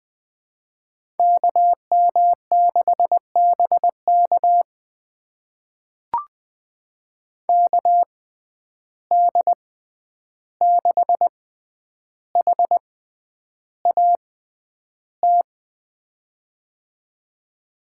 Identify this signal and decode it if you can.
Morse code, heard as a keyed tone, about 20 wpm, spaced as for 10 wpm, 700 Hz. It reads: KM6BK KD6HAT
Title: USA Call Signs - 6x Character Spacing - 20wpm